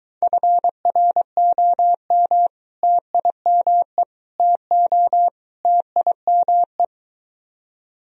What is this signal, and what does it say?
Morse code, heard as a keyed tone, about 22 wpm, 700 Hz. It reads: FROM TIME TO TIME